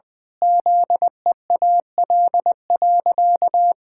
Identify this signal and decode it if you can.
Morse code, heard as a keyed tone, about 20 wpm, 700 Hz. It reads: ZEAL.
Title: Pangram Challenge v2 - 20wpm